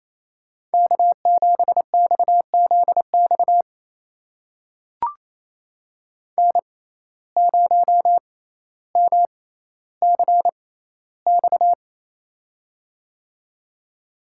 Morse code, heard as a keyed tone, about 28 wpm, spaced as for 14 wpm, 700 Hz. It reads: K7XZX N0MCX
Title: USA Call Signs - 5 Characters - 6x Character Spacing - 28wpm